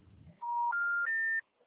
Special information tone.